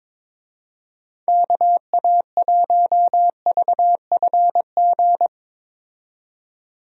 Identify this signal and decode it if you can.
Morse code, heard as a keyed tone, about 22 wpm, 700 Hz. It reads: KA1VFG
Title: USA Call Signs - 6 Characters - 22wpm